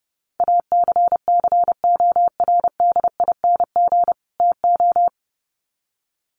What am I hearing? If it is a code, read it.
Morse code, heard as a keyed tone, about 30 wpm, 700 Hz. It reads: ACCORDING TO
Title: Sets of 2 Words 30wpm